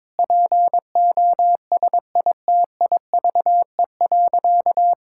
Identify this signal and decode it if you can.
Morse code, heard as a keyed tone, about 22 wpm, 700 Hz. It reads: POSITIVE.
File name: practice / 22wpm / Sentences with 8-Word Phrases - 22wpm